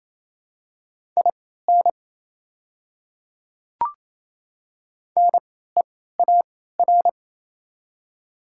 Morse code, heard as a keyed tone, about 28 wpm, spaced as for 16 wpm, 700 Hz. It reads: IN NEAR
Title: 3x Character Spacing - Top 300 Words - 28wpm